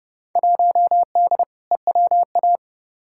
Morse code, heard as a keyed tone, about 32 wpm, 700 Hz. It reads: 1D EWA
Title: Field Day Exchanges - 30wpm